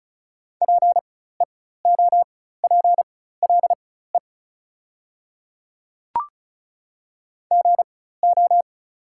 Morse code, heard as a keyed tone, about 36 wpm, spaced as for 16 wpm, 690 Hz. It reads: PEOPLE GO
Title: 4x Character Spacing - Top 500 Words - 35wpm